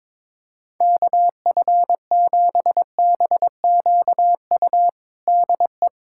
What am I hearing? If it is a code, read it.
Morse code, heard as a keyed tone, about 22 wpm, 700 Hz. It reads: KF7BQU DE
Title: USA Callsign DE Callsign 22wpm